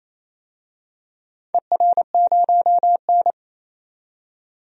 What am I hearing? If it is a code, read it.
Morse code, heard as a keyed tone, about 28 wpm, 700 Hz. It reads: ER0N
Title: DXCC-100 Call Signs - 4 Characters - 28wpm